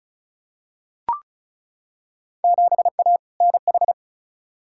Morse code, heard as a keyed tone, about 36 wpm, 700 Hz. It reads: 7A NH
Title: Field Day Exchanges - 35wpm